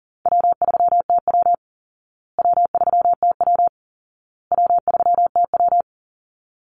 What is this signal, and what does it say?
Morse code, heard as a keyed tone, about 40 wpm, 700 Hz. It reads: W3TW W3TW W3TW
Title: Easy Call Signs Repeated 3x - 40wpm